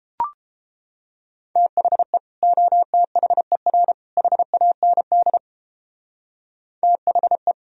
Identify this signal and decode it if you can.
Morse code, heard as a keyed tone, about 31 wpm, 700 Hz. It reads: THE OTHER HAND THE